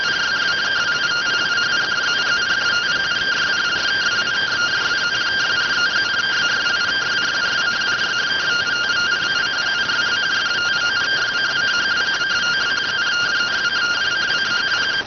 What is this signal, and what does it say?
Two-tone teleprinter signal, about 170 Hz shift.